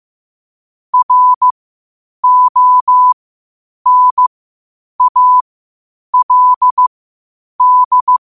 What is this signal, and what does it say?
Morse code, heard as a keyed tone, about 15 wpm, spaced as for 8 wpm, 1000 Hz. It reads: RONALD
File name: qotdfiles / 8wpm